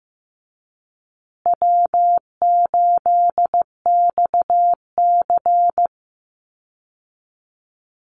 Morse code, heard as a keyed tone, about 15 wpm, 700 Hz. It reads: W8XC
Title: Easy Call Signs Repeated 2x - 15wpm